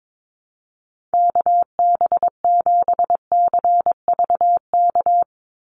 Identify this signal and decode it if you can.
Morse code, heard as a keyed tone, about 22 wpm, 700 Hz. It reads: KB7CVK
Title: USA Call Signs 22wpm